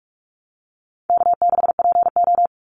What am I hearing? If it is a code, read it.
Morse code, heard as a keyed tone, about 44 wpm, 700 Hz. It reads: K6PO
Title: Call Signs Repeated 3x - 45wpm